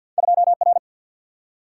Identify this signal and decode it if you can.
Morse code, heard as a keyed tone, about 49 wpm, 700 Hz. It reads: 2R
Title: Single-Letter Single-Number 50wpm